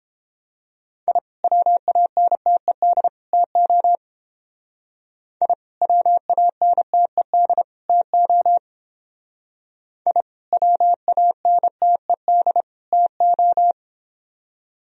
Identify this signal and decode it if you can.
Morse code, heard as a keyed tone, about 28 wpm, 700 Hz. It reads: I WANTED TO I WANTED TO I WANTED TO